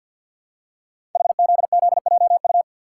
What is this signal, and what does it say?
Morse code, heard as a keyed tone, about 50 wpm, 700 Hz. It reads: HZZJU
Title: Code Groups - 5 Letters - 50wpm